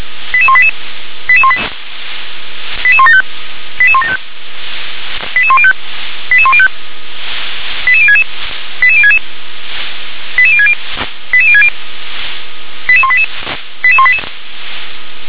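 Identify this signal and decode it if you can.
Five-tone selective call (ZVEI), heard as quick runs of five tones, tones of about 70 ms. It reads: 88188 88175 88175 88105 88105 88066 88066 88060 88060 88188 88188
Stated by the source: ZVEI five-tone call received on SDR.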